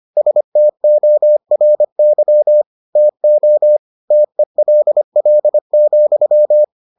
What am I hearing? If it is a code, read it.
Morse code, heard as a keyed tone, about 25 wpm, 600 Hz. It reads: STORY TO TELL,